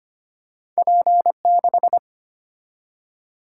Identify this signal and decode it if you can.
Morse code, heard as a keyed tone, about 25 wpm, 700 Hz. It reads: P6
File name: Two Letter-Number Combinations - 25wpm